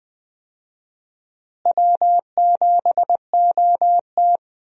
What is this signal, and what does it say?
Morse code, heard as a keyed tone, about 20 wpm, 700 Hz. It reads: W7OT
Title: USA Call Signs - 4 Characters - Repeated 2x - 20wpm